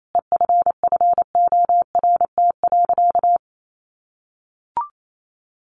Morse code, heard as a keyed tone, about 28 wpm, 700 Hz. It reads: EFFORT.
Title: Sentences from Top 1200 Words 28wpm